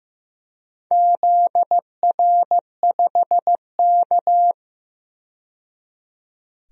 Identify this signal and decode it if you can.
Morse code, heard as a keyed tone, about 15 wpm, 700 Hz. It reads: ZR5K
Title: DXCC-100 Call Signs - 4 Characters - 15wpm